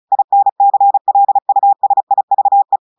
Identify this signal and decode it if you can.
Morse code, heard as a keyed tone, about 36 wpm, 800 Hz. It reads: INCLUSIVE
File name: audios / portugais / portugais texte en clair 2 vitesse 35 mots minutes